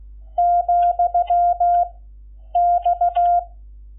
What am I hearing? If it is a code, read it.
Morse code, heard as a keyed tone, about 16 wpm, 700 Hz. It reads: , X
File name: Lecciones / CLASE Nº11